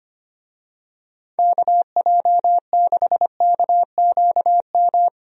Morse code, heard as a keyed tone, about 25 wpm, 700 Hz. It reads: KJ6KQM
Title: USA Call Signs - 6 Characters - 25wpm